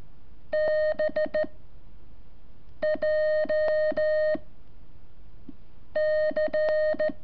Morse code, heard as a keyed tone, about 13 wpm, 620 Hz. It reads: B J C
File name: curso / ejercicio 25